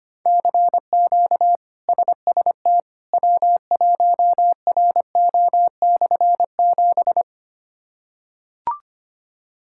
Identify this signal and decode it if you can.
Morse code, heard as a keyed tone, about 25 wpm, 700 Hz. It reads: CQ SST W1RO/7
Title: K1USN SST QSO Practice - 25wpm